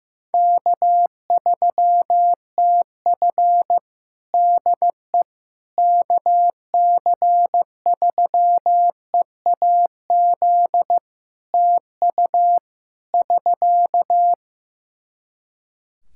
Morse code, heard as a keyed tone, about 15 wpm, 700 Hz. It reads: K3TF DE KC3EAZ TU <SK>
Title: Practice QSO - Exchange 4 - Ending - 15wpm